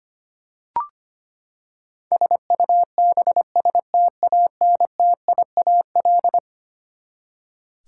Morse code, heard as a keyed tone, about 25 wpm, 700 Hz. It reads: SUBSTANTIAL
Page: morse-code-ninja-practice-25wpm